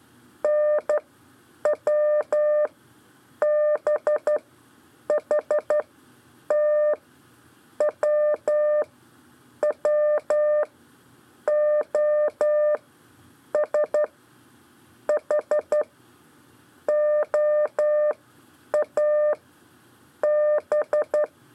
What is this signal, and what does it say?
Morse code, heard as a keyed tone, about 11 wpm, spaced as for 8 wpm, 600 Hz. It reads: NW B H T W W O S H OA B
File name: 4 LEZIONE CW 1